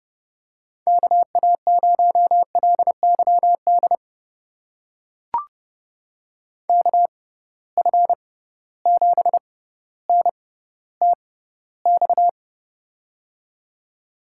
Morse code, heard as a keyed tone, about 32 wpm, spaced as for 16 wpm, 700 Hz. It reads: KA0LYD KF7NTX
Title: USA Call Signs - 6 Characters - 6x Character Spacing - 30wpm